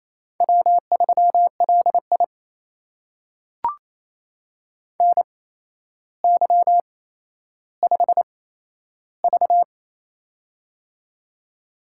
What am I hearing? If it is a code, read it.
Morse code, heard as a keyed tone, about 29 wpm, spaced as for 12 wpm, 700 Hz. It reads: W3LI NY5V
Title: USA Call Signs - 4 Characters - 8x Character Spacing - 28wpm